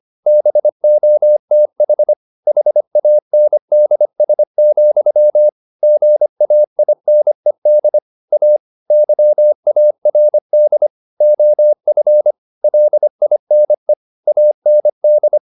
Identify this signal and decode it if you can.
Morse code, heard as a keyed tone, about 25 wpm, 600 Hz. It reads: BOTH HANDS, GAINED A YARD OF LINE AND
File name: the old man and the sea part 1 25wpm